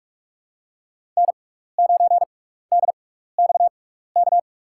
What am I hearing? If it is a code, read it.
Morse code, heard as a keyed tone, about 46 wpm, spaced as for 15 wpm, 700 Hz. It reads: N9DXK
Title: USA Call Signs - 5 Characters - 6x Character Spacing - 45wpm